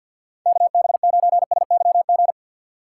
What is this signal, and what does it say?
Morse code, heard as a keyed tone, about 47 wpm, 700 Hz. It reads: KB9SYG
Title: USA Call Signs - 2x Character Spacing - 50wpm